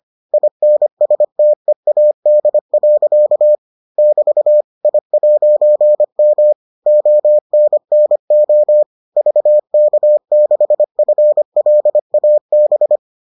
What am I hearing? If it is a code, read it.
Morse code, heard as a keyed tone, about 25 wpm, 600 Hz. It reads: INSTEAD. <BT> I'M ONNO VK6FLAB